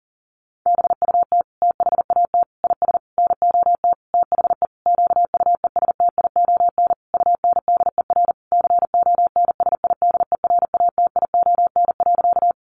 Morse code, heard as a keyed tone, about 40 wpm, 700 Hz. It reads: BUT THAT IS NOT THE QUESTION UNDER CONSIDERATION.
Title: Sentences from Top 1500 Words 40wpm